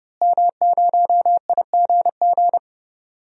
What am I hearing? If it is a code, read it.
Morse code, heard as a keyed tone, about 31 wpm, 700 Hz. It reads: M0IGG
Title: DXCC-100 Call Signs - 5 Characters - Repeated 2x - 30wpm